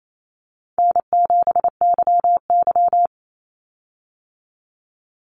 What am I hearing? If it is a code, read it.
Morse code, heard as a keyed tone, about 28 wpm, 700 Hz. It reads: N7YY
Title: Call Signs Repeated 2x - 28wpm